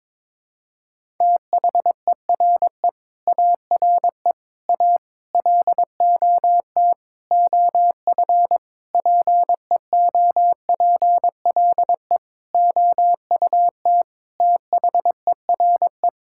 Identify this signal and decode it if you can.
Morse code, heard as a keyed tone, about 22 wpm, 700 Hz. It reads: THERE ARE A LOT OF PEOPLE OUT THERE
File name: practice / 22wpm / Sets of 8 Words - 22wpm